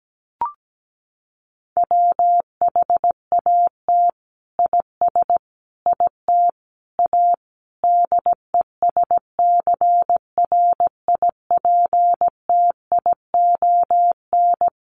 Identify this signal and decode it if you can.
Morse code, heard as a keyed tone, about 17 wpm, 700 Hz. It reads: WHAT IS IT A DESCRIPTION
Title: Sentences from Top 1300 Words 17wpm